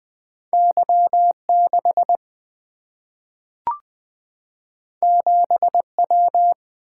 Morse code, heard as a keyed tone, about 20 wpm, 700 Hz. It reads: Y 6 7 W
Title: Single-Letter Single-Number 20wpm